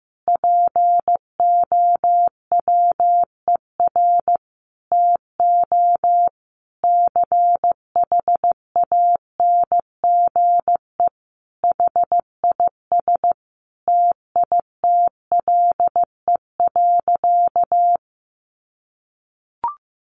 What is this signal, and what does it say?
Morse code, heard as a keyed tone, about 15 wpm, 700 Hz. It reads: POWER TO CHANGE HIS TITLE.
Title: Sentences from Top 1200 Words 15wpm